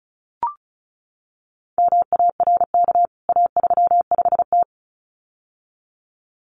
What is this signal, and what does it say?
Morse code, heard as a keyed tone, about 34 wpm, 700 Hz. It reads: MARK A35T
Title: CWT Exchange 35wpm